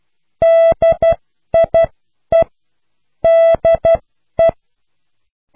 Morse code, heard as a keyed tone, about 12 wpm, spaced as for 10 wpm, 660 Hz. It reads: DIE DE